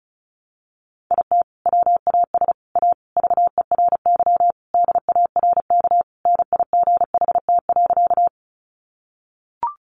Morse code, heard as a keyed tone, about 34 wpm, 700 Hz. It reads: IT WAS A VERY DARK NIGHT.
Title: Sentences from Top 500 Words 35wpm